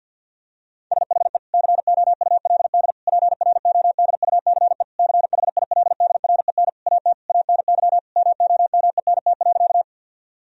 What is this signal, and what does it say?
Morse code, heard as a keyed tone, about 50 wpm, 700 Hz. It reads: SHE COULD PRODUCE CHILDREN AT ANY MOMENT.
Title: Sentences from Top 1000 Words - Review Part 3 - 50wpm